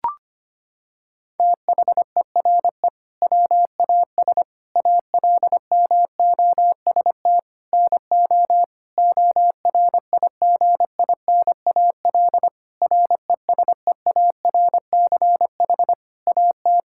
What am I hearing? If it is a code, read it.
Morse code, heard as a keyed tone, about 25 wpm, 700 Hz. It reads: THERE WAS ALMOST NO ORIGINAL RESEARCH AT
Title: Sentences from Top 1000 Words - Review Part 6 - 25wpm